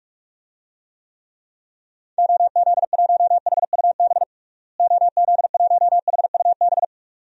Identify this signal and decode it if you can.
Morse code, heard as a keyed tone, about 42 wpm, 700 Hz. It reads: OZ1HUB OZ1HUB
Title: DXCC-100 Call Signs - 6 Characters - Repeated 2x - 45wpm